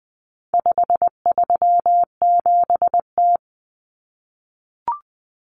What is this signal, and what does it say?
Morse code, heard as a keyed tone, about 20 wpm, 700 Hz. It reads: 537T
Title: Common Cut Numbers - 4-Digit 20wpm